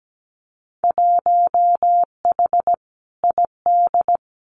Morse code, heard as a keyed tone, about 17 wpm, 700 Hz. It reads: 1H ID
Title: Winter Field Day Exchanges 17wpm